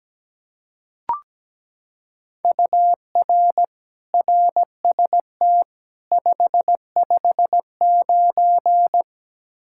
Morse code, heard as a keyed tone, about 17 wpm, 700 Hz. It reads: UR RST 559
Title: Practice QSO - Exchange 1 - Beginning - 17wpm